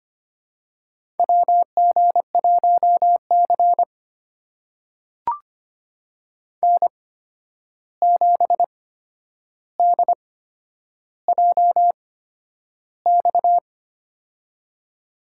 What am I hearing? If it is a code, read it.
Morse code, heard as a keyed tone, about 25 wpm, spaced as for 10 wpm, 700 Hz. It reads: WG1C N7DJX
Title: USA Call Signs - 8x Character Spacing - 25wpm